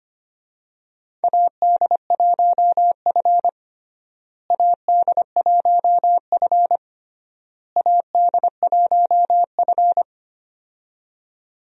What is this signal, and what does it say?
Morse code, heard as a keyed tone, about 25 wpm, 700 Hz. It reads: AD1F AD1F AD1F
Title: USA Call Signs - 4 Characters - Repeated 3x - 25wpm